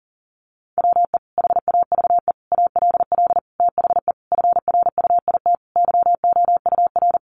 Morse code, heard as a keyed tone, about 40 wpm, 700 Hz. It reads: WE HAVE ALL THE FRUIT YOUR
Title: Sentences from Top 1300 Words 40wpm